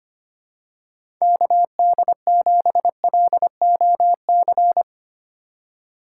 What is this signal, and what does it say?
Morse code, heard as a keyed tone, about 25 wpm, 700 Hz. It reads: KD7LOC
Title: USA Call Signs - 6 Characters - 2x Character Spacing - 25wpm